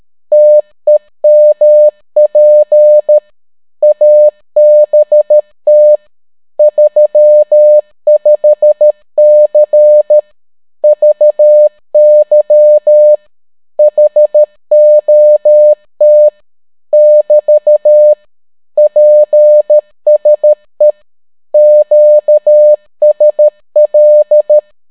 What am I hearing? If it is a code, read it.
Morse code, heard as a keyed tone, about 13 wpm, 600 Hz. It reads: TEMP ABT 35C VY HOT <BT> PSE QSL